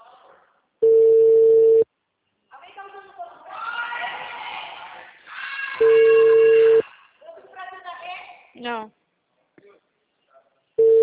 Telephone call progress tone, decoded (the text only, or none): ringback tone